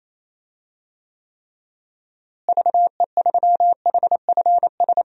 Morse code, heard as a keyed tone, about 27 wpm, 700 Hz. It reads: VE3HFS
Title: DXCC-100 Call Signs - 6 Characters - Repeated 3x - 28wpm